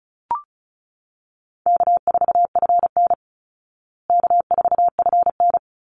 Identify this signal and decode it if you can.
Morse code, heard as a keyed tone, about 34 wpm, 700 Hz. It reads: K4FN K4FN
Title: Easy Call Signs Repeated 2x - 35wpm